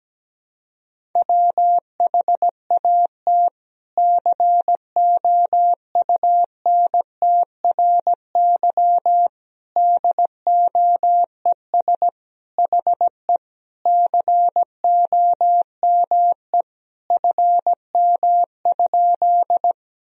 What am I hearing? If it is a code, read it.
Morse code, heard as a keyed tone, about 17 wpm, 700 Hz. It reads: WHAT COUNTRY DOES HE COME FM?
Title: Sentences from Top 300 Words and Abbreviations 17wpm